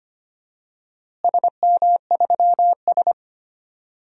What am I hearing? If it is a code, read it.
Morse code, heard as a keyed tone, about 25 wpm, 700 Hz. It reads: SM3S
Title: DXCC-100 Call Signs - 4 Characters - Repeated 3x - 25wpm